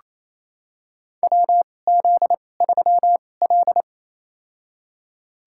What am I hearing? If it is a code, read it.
Morse code, heard as a keyed tone, about 28 wpm, spaced as for 20 wpm, 700 Hz. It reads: WZ3L
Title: USA Call Signs - 4 Characters - 2x Character Spacing - 28wpm